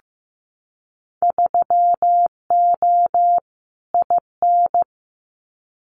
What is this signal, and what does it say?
Morse code, heard as a keyed tone, about 15 wpm, 700 Hz. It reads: 3O IN
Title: Winter Field Day Exchanges Encore 15wpm